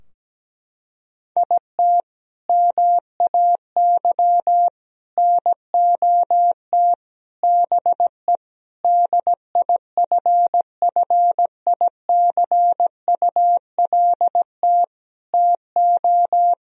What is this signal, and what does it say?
Morse code, heard as a keyed tone, about 17 wpm, 700 Hz. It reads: IT MAY NOT BE DIFFICULT TO
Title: Sentences from Top 1000 Words - Review Part 4 - 17wpm